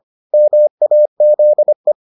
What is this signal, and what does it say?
Morse code, heard as a keyed tone, about 25 wpm, 600 Hz. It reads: MAZE